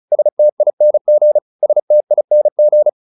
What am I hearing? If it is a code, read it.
Morse code, heard as a keyed tone, about 34 wpm, 600 Hz. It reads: STING STING